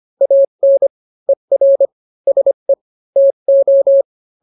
Morse code, heard as a keyed tone, about 25 wpm, 550 Hz. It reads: AN ER SE TO